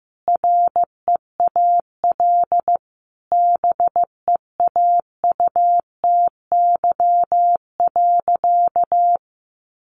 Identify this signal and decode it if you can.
Morse code, heard as a keyed tone, about 15 wpm, 700 Hz. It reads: REAL BEAUTY.
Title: Sentences from Top 1100 Words 15wpm